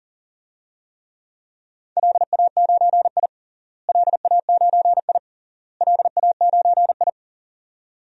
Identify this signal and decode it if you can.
Morse code, heard as a keyed tone, about 40 wpm, 700 Hz. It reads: LA9I LA9I LA9I
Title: DXCC-100 Call Signs - 4 Characters - Repeated 3x - 40wpm